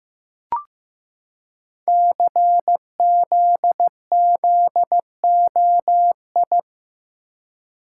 Morse code, heard as a keyed tone, about 15 wpm, 700 Hz. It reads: CZZOI
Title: Code Groups - 5 Letters - 15wpm